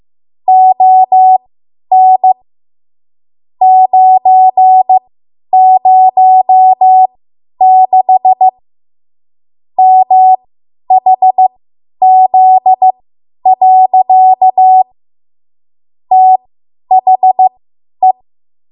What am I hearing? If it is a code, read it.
Morse code, heard as a keyed tone, about 15 wpm, spaced as for 10 wpm, 750 Hz. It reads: ON 906 MHZ. THE
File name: Morse / 10WPM